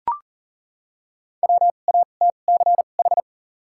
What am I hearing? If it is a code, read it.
Morse code, heard as a keyed tone, about 40 wpm, 700 Hz. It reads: W A T C H
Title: 2x Character Spacing - Top 300 Words - 40wpm